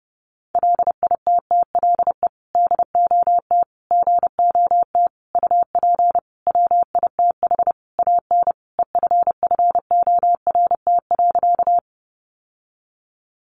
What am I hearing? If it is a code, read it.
Morse code, heard as a keyed tone, about 30 wpm, 700 Hz. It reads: LITTLE DOT GOT UP WITH AN EFFORT.
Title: Sentences from Top 1500 Words 30wpm